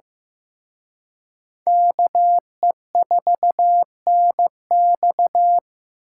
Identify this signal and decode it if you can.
Morse code, heard as a keyed tone, about 18 wpm, 700 Hz. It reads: KE4NX
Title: Speed Racing - USA Call Signs 15wpm